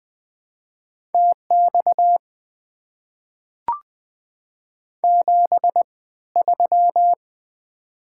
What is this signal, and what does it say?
Morse code, heard as a keyed tone, about 20 wpm, spaced as for 14 wpm, 690 Hz. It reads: TX 73
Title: Top 90 QSO Elements - 3x Character Spacing - 20wpm